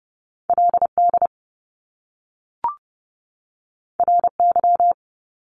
Morse code, heard as a keyed tone, about 30 wpm, 700 Hz. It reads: LD RY
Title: Top Two Letter Combinations in Words 30wpm